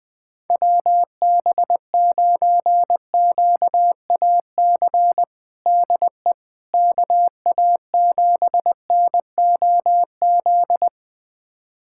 Morse code, heard as a keyed tone, about 20 wpm, 700 Hz. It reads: WB9QAC DE KA7NOZ
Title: USA Callsign DE Callsign - 6 Characters - 20wpm